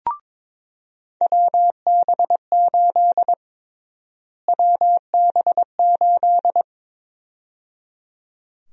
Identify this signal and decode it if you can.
Morse code, heard as a keyed tone, about 22 wpm, 700 Hz. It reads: WB8 WB8
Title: USA Call Sign Prefixes - Repeated 2x - 22wpm